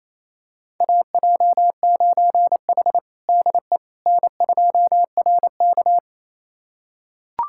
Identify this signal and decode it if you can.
Morse code, heard as a keyed tone, about 28 wpm, 700 Hz. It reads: AJ9H DE N2RK
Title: USA Callsign DE Callsign - 4 Characters - 28wpm